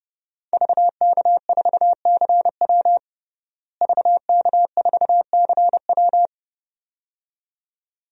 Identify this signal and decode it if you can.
Morse code, heard as a keyed tone, about 30 wpm, 700 Hz. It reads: VK4CW VK4CW
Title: DXCC-100 Call Signs - 5 Characters - Repeated 2x - 30wpm